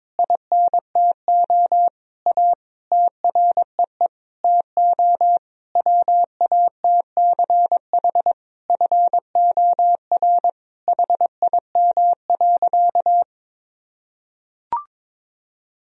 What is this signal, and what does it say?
Morse code, heard as a keyed tone, about 22 wpm, 700 Hz. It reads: INTO A TREE TO WATCH FOR HIM.
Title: Sentences from Top 1000 Words - Review Part 1 - 22wpm